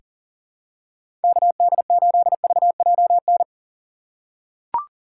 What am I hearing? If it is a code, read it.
Morse code, heard as a keyed tone, about 39 wpm, 700 Hz. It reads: KD8VJN
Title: USA Call Signs - 6 Characters - 2x Character Spacing - 40wpm